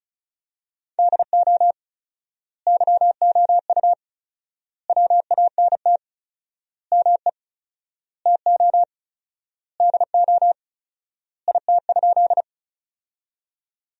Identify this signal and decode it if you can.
Morse code, heard as a keyed tone, about 34 wpm, 700 Hz. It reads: DO YOU WANT ME TO DO IT?
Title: 4x Word-Spacing - Sentences from Top 200 Words - 35wpm